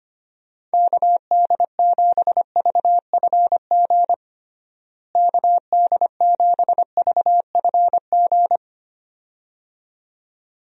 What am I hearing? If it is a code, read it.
Morse code, heard as a keyed tone, about 25 wpm, 700 Hz. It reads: KD7VFG KD7VFG
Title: USA Call Signs - 6 Characters - Repeated 2x - 25wpm